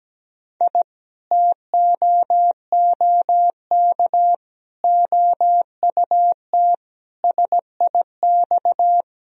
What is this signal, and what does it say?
Morse code, heard as a keyed tone, about 17 wpm, 700 Hz. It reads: I TOOK OUT SIX
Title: Sentences from Top 1000 Words - Review Part 2 - 17wpm